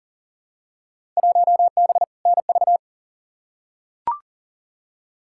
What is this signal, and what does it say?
Morse code, heard as a keyed tone, about 40 wpm, 700 Hz. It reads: 1B NV
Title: Field Day Exchanges - 40wpm